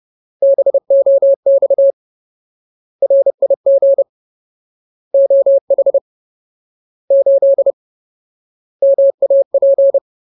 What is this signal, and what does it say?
Morse code, heard as a keyed tone, about 30 wpm, 550 Hz. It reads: BOX RIG OH 8 MAP
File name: Ultimate Word List / 30WPM Ultimate